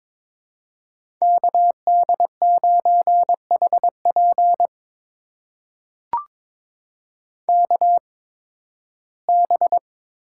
Morse code, heard as a keyed tone, about 22 wpm, spaced as for 14 wpm, 700 Hz. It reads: KD9HP KB
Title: USA Call Signs - 8x Character Spacing - 22wpm